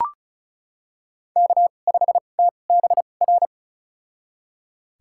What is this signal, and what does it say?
Morse code, heard as a keyed tone, about 36 wpm, spaced as for 25 wpm, 700 Hz. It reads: K5TBR
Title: USA Call Signs - 5 Characters - 2x Character Spacing - 35wpm